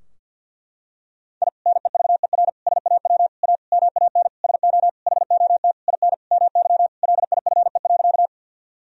Morse code, heard as a keyed tone, about 47 wpm, 700 Hz. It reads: I NEVER SAW A MAN SO HOT IN MY LIFE.
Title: Sentences from Top 1000 Words - Review Part 3 - 50wpm